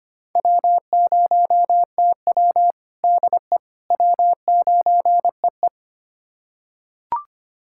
Morse code, heard as a keyed tone, about 25 wpm, 700 Hz. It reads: W0TW DE W9EE
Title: USA Callsign DE Callsign - 4 Characters - 25wpm